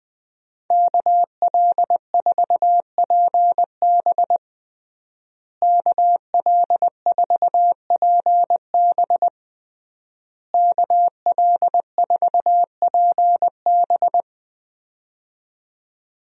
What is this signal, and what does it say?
Morse code, heard as a keyed tone, about 20 wpm, 700 Hz. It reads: KL4PB KL4PB KL4PB
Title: USA Call Signs - 5 Characters - Repeated 3x - 20wpm